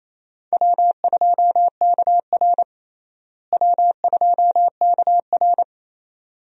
Morse code, heard as a keyed tone, about 28 wpm, 700 Hz. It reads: W2KR W2KR
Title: USA Call Signs - 4 Characters - Repeated 2x - 28wpm